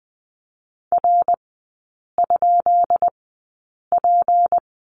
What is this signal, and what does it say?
Morse code, heard as a keyed tone, about 20 wpm, spaced as for 8 wpm, 700 Hz. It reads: R?P
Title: 152 Lesson 25e - Letter p - Three Characters 20wpm